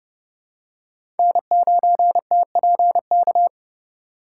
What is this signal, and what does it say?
Morse code, heard as a keyed tone, about 32 wpm, 700 Hz. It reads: N9TPK
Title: USA Call Signs - 5 Characters - 2x Character Spacing - 30wpm